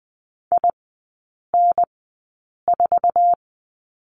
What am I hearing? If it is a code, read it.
Morse code, heard as a keyed tone, about 20 wpm, spaced as for 8 wpm, 700 Hz. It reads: IN4
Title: 060 Lesson 11e - Letter h - Three Characters 20wpm